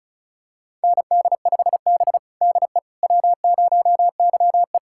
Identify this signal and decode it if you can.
Morse code, heard as a keyed tone, about 36 wpm, 700 Hz. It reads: ND5B DE W0YE
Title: USA Callsign DE Callsign - 4 Characters - 35wpm